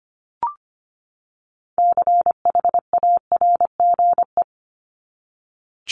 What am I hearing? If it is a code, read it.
Morse code, heard as a keyed tone, about 25 wpm, 700 Hz. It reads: CHARGE
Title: Top 751-1000 Words 25wpm